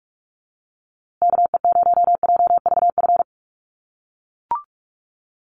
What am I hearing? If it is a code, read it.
Morse code, heard as a keyed tone, about 44 wpm, 690 Hz. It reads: KE0JVF
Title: USA Call Signs Repeated 2x 45wpm